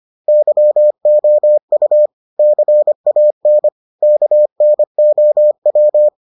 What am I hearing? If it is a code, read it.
Morse code, heard as a keyed tone, about 25 wpm, 600 Hz. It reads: YOU CAN KNOW